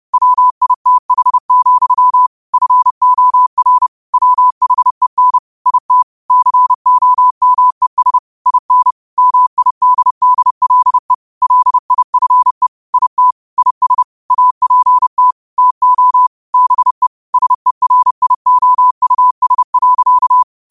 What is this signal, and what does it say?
Morse code, heard as a keyed tone, about 30 wpm, 1000 Hz. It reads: WITH, FOR WHEN IT COMES IN MIDDLE LIFE IT IS APT TO BE SERIOUS.